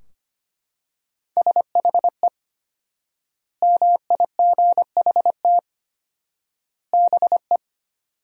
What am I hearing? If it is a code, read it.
Morse code, heard as a keyed tone, about 25 wpm, 700 Hz. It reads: SHE MIGHT BE
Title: 4x Word-Spacing - Sentences from Top 300 Words - 25wpm